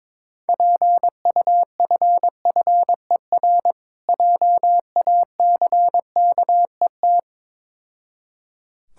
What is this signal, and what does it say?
Morse code, heard as a keyed tone, about 22 wpm, 690 Hz. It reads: PUFFER JACKET
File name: Difficult Words with Hints - 22wpm